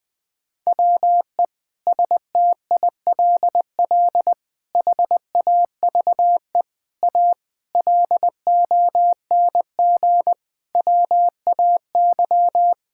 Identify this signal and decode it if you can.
Morse code, heard as a keyed tone, about 21 wpm, 700 Hz. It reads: WE STILL HAVE A LONG WAY
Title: Sentences with 6-Word Phrases - 20wpm